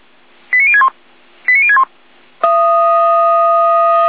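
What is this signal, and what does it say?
Five-tone selective call (ZVEI), heard as quick runs of five tones, tones of about 70 ms. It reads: 89061 89061
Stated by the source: ZVEI five-tone call 89060 received by SDR.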